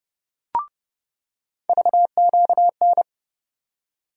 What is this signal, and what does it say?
Morse code, heard as a keyed tone, about 30 wpm, 700 Hz. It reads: VQN
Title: USA Call Sign Postfixes - 3 Letters 30wpm